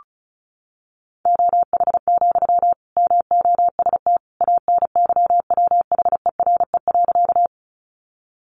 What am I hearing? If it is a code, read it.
Morse code, heard as a keyed tone, about 34 wpm, 700 Hz. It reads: OH, MOST ANYWHERE.
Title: Sentences from Top 1400 Words 35wpm